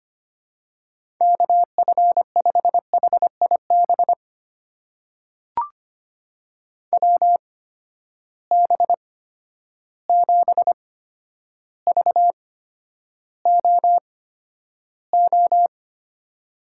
Morse code, heard as a keyed tone, about 25 wpm, spaced as for 11 wpm, 700 Hz. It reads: KF5HIB WB7VOO
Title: USA Call Signs - 6 Characters - 8x Character Spacing - 25wpm